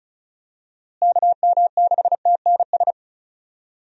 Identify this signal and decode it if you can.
Morse code, heard as a keyed tone, about 36 wpm, 700 Hz. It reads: KM6TNS
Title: USA Call Signs - 6 Characters - Repeated 2x - 35wpm